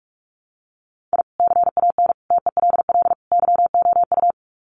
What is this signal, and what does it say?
Morse code, heard as a keyed tone, about 44 wpm, 700 Hz. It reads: I CAN TELL YOU
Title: Sets of 4 Words 45wpm